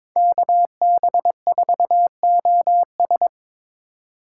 Morse code, heard as a keyed tone, about 26 wpm, 700 Hz. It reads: KB4OS
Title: Speed Racing - USA 2x2 Call Signs 22wpm